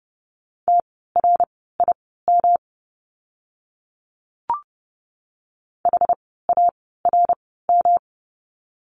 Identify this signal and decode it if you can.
Morse code, heard as a keyed tone, about 30 wpm, spaced as for 17 wpm, 700 Hz. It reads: TRIM HARM
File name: course / 30:10wpm / 108 Lesson 18f - Letter m - Words 30wpm